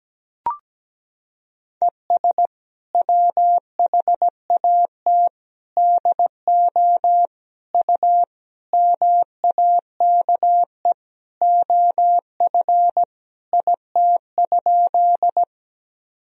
Morse code, heard as a keyed tone, about 17 wpm, 700 Hz. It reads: ES WHAT DO U MAKE OF IT?
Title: Sentences from Top 100 Words and Abbreviations 17wpm